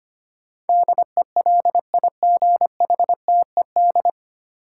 Morse code, heard as a keyed tone, about 25 wpm, 700 Hz. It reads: DELIGHTED
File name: Nine Letter Words - 25wpm